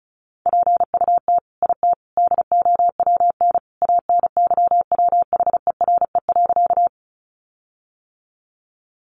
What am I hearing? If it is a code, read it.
Morse code, heard as a keyed tone, about 34 wpm, 700 Hz. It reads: PUT IT DOWN ANYWHERE.
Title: Sentences from Top 1400 Words 35wpm